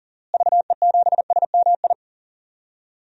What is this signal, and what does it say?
Morse code, heard as a keyed tone, about 40 wpm, 700 Hz. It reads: VE7SMI
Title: DXCC-100 Call Signs - 6 Characters - Repeated 2x - 40wpm